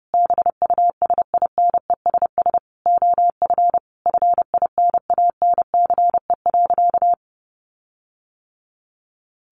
Morse code, heard as a keyed tone, about 30 wpm, 700 Hz. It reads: BUSINESS OF FINANCE.
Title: Sentences from Top 1500 Words 30wpm